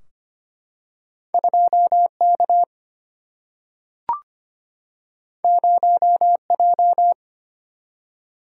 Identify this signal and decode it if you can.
Morse code, heard as a keyed tone, about 25 wpm, 700 Hz. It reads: 2 K 0 J
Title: Single-Letter Single-Number 25wpm